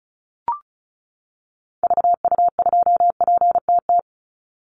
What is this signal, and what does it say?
Morse code, heard as a keyed tone, about 34 wpm, 700 Hz. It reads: VU2PTT
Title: Call Signs - 35wpm